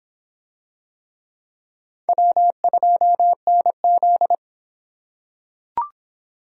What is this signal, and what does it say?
Morse code, heard as a keyed tone, about 24 wpm, 700 Hz. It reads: W2NZ